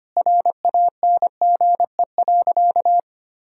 Morse code, heard as a keyed tone, about 25 wpm, 700 Hz. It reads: RANGE.
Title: Sentences from Top 1000 Words - Review Part 4 - 25wpm